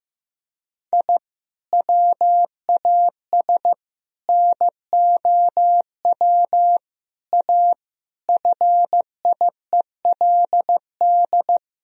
Morse code, heard as a keyed tone, about 15 wpm, 700 Hz. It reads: I WAS NOW A FIELD
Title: Sentences from Top 1000 Words - Review Part 2 - 15wpm